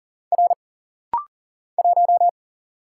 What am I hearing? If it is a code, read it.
Morse code, heard as a keyed tone, about 40 wpm, 700 Hz. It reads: R 1
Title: Single Letter-Number Alt - Rapid-Fire 40wpm